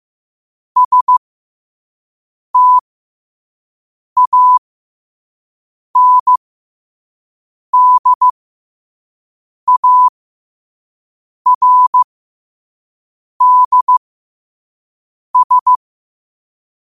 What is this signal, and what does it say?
Morse code, heard as a keyed tone, about 15 wpm, spaced as for 5 wpm, 1000 Hz. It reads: STANDARDS
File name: qotdfiles / 5wpm